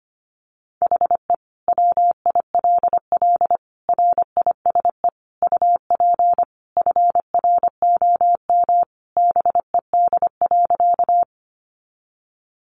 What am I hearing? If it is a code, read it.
Morse code, heard as a keyed tone, about 25 wpm, 700 Hz. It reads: HE WILL RISE UP FROM BED.
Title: Sentences from Top 900 Words 25wpm